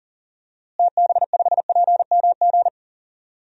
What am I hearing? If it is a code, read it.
Morse code, heard as a keyed tone, about 40 wpm, 700 Hz. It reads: TB5PMG
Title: DXCC-100 Call Signs - 6 Characters - Repeated 3x - 40wpm